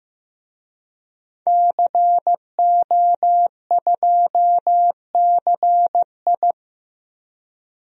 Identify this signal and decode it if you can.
Morse code, heard as a keyed tone, about 15 wpm, 700 Hz. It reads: CO2CI
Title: DXCC-100 Call Signs - 5 Characters - 15wpm